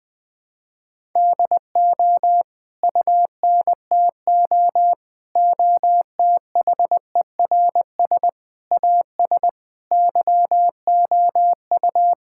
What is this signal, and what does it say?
Morse code, heard as a keyed tone, about 20 wpm, 700 Hz. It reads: DO UNTO OTHERS AS YOU
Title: Sentences with 9-Word Phrases - 20wpm